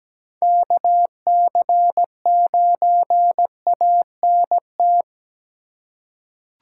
Morse code, heard as a keyed tone, about 17 wpm, 700 Hz. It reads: KC9ANT
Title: USA Call Signs - 4x Character Spacing - 17wpm